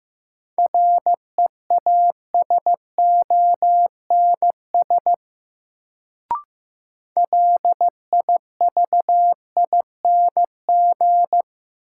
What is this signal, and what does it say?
Morse code, heard as a keyed tone, about 15 wpm, 700 Hz. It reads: REASONS LIVING
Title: Sets of 4 Words - New Word Drill - 15wpm